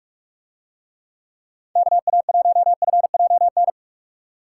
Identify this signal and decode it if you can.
Morse code, heard as a keyed tone, about 43 wpm, 700 Hz. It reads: KA1FJN